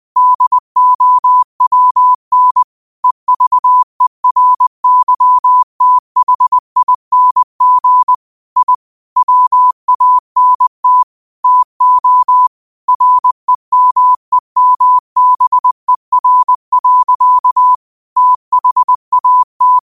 Morse code, heard as a keyed tone, about 20 wpm, 1000 Hz. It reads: DOWN EVERYTHING I WANT TO REMEMBER. THAT
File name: qotdfiles / 20wpm